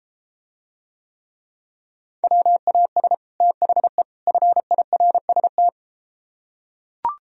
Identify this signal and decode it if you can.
Morse code, heard as a keyed tone, about 33 wpm, 700 Hz. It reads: WAS THE FIRST